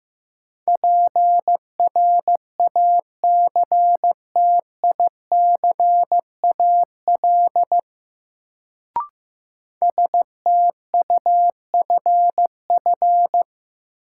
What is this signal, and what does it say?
Morse code, heard as a keyed tone, about 15 wpm, 700 Hz. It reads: PRACTICAL STUFF
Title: Sentences from Top 1400 Words - New Word Drill - 15wpm